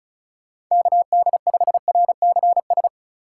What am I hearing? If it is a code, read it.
Morse code, heard as a keyed tone, about 38 wpm, 700 Hz. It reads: KD5RCS
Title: Speed Racing - USA 2x3 Call Signs 30wpm